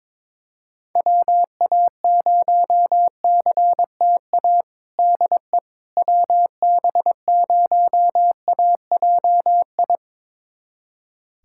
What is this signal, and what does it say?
Morse code, heard as a keyed tone, about 22 wpm, 700 Hz. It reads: WA0CTA DE WB0AJI
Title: USA Callsign DE Callsign - 6 Characters - 22wpm